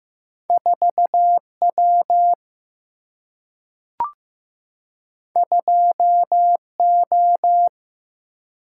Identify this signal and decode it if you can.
Morse code, heard as a keyed tone, about 15 wpm, 700 Hz. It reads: 4 W 2 O
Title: Single-Letter Single-Number 15wpm